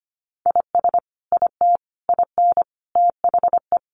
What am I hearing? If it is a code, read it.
Morse code, heard as a keyed tone, about 25 wpm, 700 Hz. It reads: IS IT IN THE
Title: Sentences from Top 1200 Words 25wpm